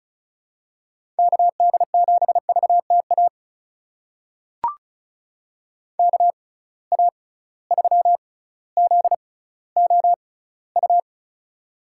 Morse code, heard as a keyed tone, about 36 wpm, spaced as for 18 wpm, 700 Hz. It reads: KD7VTA KA3ZOU
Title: USA Call Signs - 6 Characters - 6x Character Spacing - 35wpm